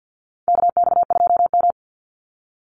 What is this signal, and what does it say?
Morse code, heard as a keyed tone, about 48 wpm, 700 Hz. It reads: KX2M
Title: USA Call Signs 50wpm